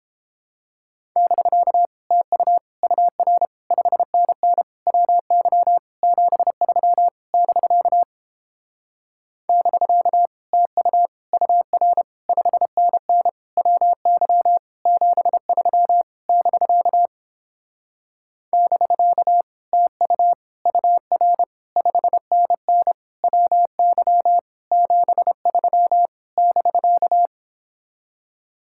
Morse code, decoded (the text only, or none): <BK> TU UR 5NN WY 73 <BK> <BK> TU UR 5NN WY 73 <BK> <BK> TU UR 5NN WY 73 <BK>